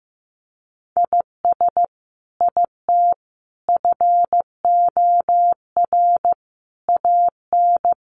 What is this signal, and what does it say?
Morse code, heard as a keyed tone, about 15 wpm, 700 Hz. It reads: IS IT FOR AN
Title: Sentences from Top 1200 Words 15wpm